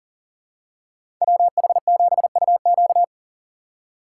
Morse code, heard as a keyed tone, about 40 wpm, 700 Hz. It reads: WH7UQ
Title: USA Call Signs - 5 Characters - 4x Character Spacing - 40wpm